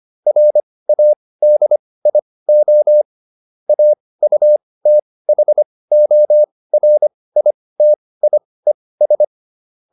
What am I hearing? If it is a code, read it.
Morse code, heard as a keyed tone, about 25 wpm, spaced as for 18 wpm, 600 Hz. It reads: RADIO AUTHORITIES